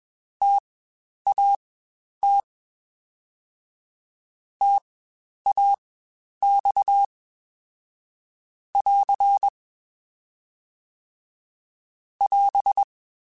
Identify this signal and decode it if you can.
Morse code, heard as a keyed tone, about 21 wpm, spaced as for 10 wpm, 800 Hz. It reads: TAT TAX <AR> <AS>
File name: Spaced Alphas / Spaced Alphas mp3s / Ts sp @20